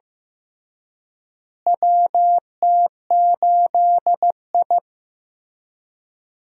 Morse code, heard as a keyed tone, about 15 wpm, 700 Hz. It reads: WT8I
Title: USA Call Signs - 4 Characters - Repeated 2x - 15wpm